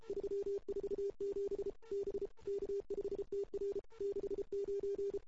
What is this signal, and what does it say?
Morse code, heard as a keyed tone, about 31 wpm, 400 Hz. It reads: 347 B K5TR 69
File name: CW / 2008-ss-cw-k5tr-op-n5rz -00:00